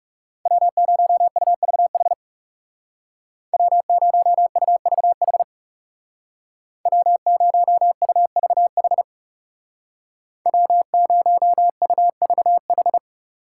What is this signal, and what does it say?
Morse code, heard as a keyed tone, about 36 wpm, 700 Hz. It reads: W0UVH W0UVH W0UVH W0UVH
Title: Speed Racing - USA 1x3 Call Signs 30wpm